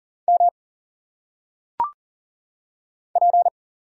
Morse code, heard as a keyed tone, about 40 wpm, 700 Hz. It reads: M P
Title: Single Letters - Phonetic Pronunciation 40wpm